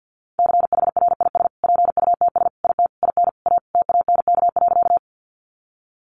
Morse code, heard as a keyed tone, about 50 wpm, 700 Hz. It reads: CHRIS PUTS IT IN A TANK.
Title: Additional Easy Phrases 50wpm